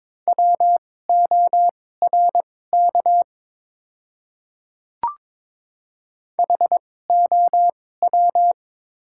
Morse code, heard as a keyed tone, about 22 wpm, spaced as for 16 wpm, 700 Hz. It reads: WORK HOW
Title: 2x Character Spacing - Top 500 Words - 22wpm